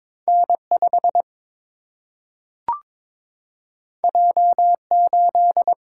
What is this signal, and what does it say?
Morse code, heard as a keyed tone, about 22 wpm, 700 Hz. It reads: N5 J8
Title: Single-Letter Single-Number 22wpm